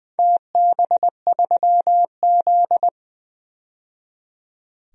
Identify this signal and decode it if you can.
Morse code, heard as a keyed tone, about 20 wpm, 700 Hz. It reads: TB3Z
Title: DXCC-100 Call Signs - 4 Characters - 20wpm